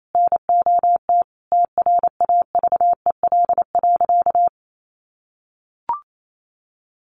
Morse code, heard as a keyed tone, about 28 wpm, 700 Hz. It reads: NOT TRAVEL.
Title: Sentences from Top 1300 Words 28wpm